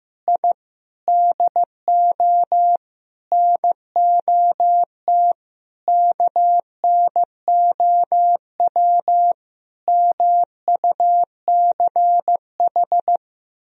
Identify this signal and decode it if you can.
Morse code, heard as a keyed tone, about 15 wpm, 700 Hz. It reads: I DO NOT KNOW MUCH
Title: Sentences from Top 1000 Words - Review Part 2 - 15wpm